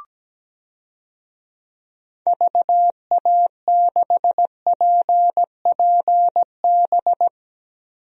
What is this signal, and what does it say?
Morse code, heard as a keyed tone, about 17 wpm, 700 Hz. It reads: VA6PPB
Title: DXCC-100 Call Signs - 6 Characters - Repeated 2x - 17wpm